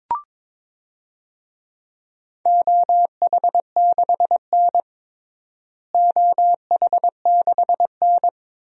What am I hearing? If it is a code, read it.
Morse code, heard as a keyed tone, about 22 wpm, 700 Hz. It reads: OH6N OH6N
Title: DXCC-100 Call Signs - 4 Characters - Repeated 3x - 22wpm